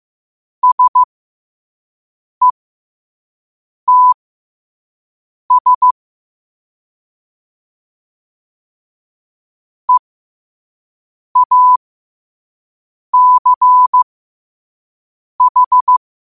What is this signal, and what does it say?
Morse code, heard as a keyed tone, about 15 wpm, spaced as for 5 wpm, 1000 Hz. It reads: SETS EACH
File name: qotdfiles / 5wpm